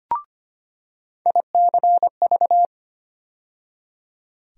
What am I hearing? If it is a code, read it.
Morse code, heard as a keyed tone, about 25 wpm, 700 Hz. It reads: ICV
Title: USA Call Sign Postfixes - 3 Letters 25wpm